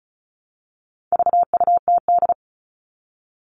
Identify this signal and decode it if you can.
Morse code, heard as a keyed tone, about 34 wpm, 700 Hz. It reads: VUTD
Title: All Cut Numbers - 4-Digit 35wpm